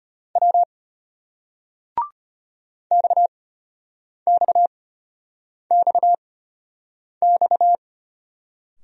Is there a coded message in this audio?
Yes: Morse code from 0.4 to 7.8 s, heard as a keyed tone, about 36 wpm, spaced as for 8 wpm, 700 Hz. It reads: W XXXX